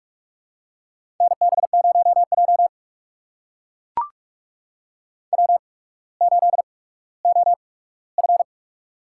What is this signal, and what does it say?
Morse code, heard as a keyed tone, about 44 wpm, spaced as for 19 wpm, 700 Hz. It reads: NB0J W8OF
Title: USA Call Signs - 4 Characters - 8x Character Spacing - 45wpm